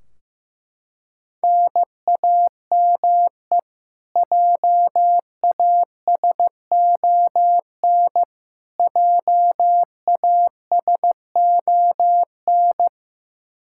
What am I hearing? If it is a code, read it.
Morse code, heard as a keyed tone, about 15 wpm, 700 Hz. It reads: NAME JASON JASON
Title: Practice QSO - Exchange 1 - Beginning - 15wpm